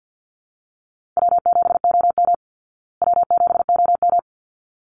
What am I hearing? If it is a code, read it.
Morse code, heard as a keyed tone, about 50 wpm, 700 Hz. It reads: W7OM W7OM
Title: Easy Call Signs Repeated 2x - 50wpm